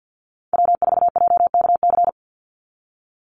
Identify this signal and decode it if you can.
Morse code, heard as a keyed tone, about 50 wpm, 700 Hz. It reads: W4JKC
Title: NA SOTA Chaser Callsigns 50wpm